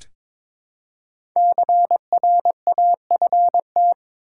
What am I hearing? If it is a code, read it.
Morse code, heard as a keyed tone, about 22 wpm, 690 Hz. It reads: CRAFT
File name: 5 Letter Words - First 3 Letters Given - 22wpm